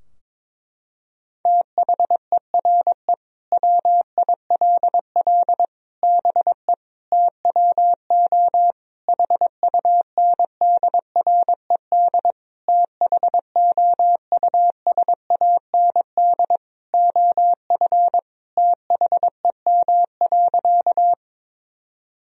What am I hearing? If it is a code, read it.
Morse code, heard as a keyed tone, about 22 wpm, 700 Hz. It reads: THERE WILL BE TWO HUNDRED THOUSAND OF THEM.
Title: Sentences from Top 1000 Words - Review Part 3 - 22wpm